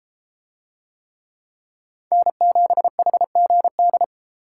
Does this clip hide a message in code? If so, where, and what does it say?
Yes: Morse code from 2.1 to 4.0 s, heard as a keyed tone, about 31 wpm, 700 Hz. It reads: N7HGD